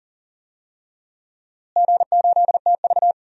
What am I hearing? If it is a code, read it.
Morse code, heard as a keyed tone, about 40 wpm, 700 Hz. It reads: G8TV
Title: DXCC-100 Call Signs - 4 Characters - Repeated 3x - 40wpm